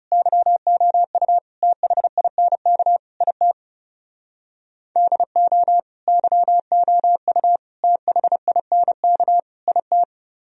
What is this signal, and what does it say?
Morse code, heard as a keyed tone, about 38 wpm, 700 Hz. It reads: YOU THINK IT DO YOU THINK IT